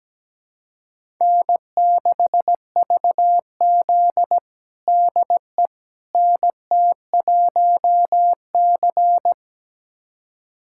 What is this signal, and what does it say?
Morse code, heard as a keyed tone, about 17 wpm, 700 Hz. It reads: N6VZ DE NT1C
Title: USA Callsign DE Callsign - 4 Characters - 17wpm